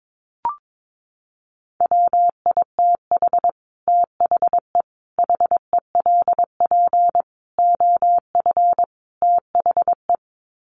Morse code, heard as a keyed tone, about 22 wpm, 700 Hz. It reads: WITH THE HELP OF THE
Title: Sets of 5 Words Encore 22wpm